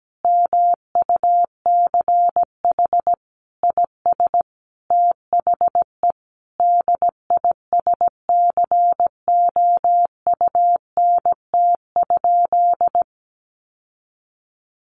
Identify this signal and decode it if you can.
Morse code, heard as a keyed tone, about 17 wpm, 700 Hz. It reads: MUCH IS THE DISCOUNT?
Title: Sentences from Top 1200 Words 17wpm